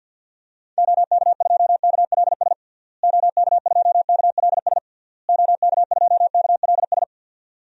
Morse code, heard as a keyed tone, about 51 wpm, 700 Hz. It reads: OK2KLS OK2KLS OK2KLS
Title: DXCC-100 Call Signs - 6 Characters - Repeated 3x - 50wpm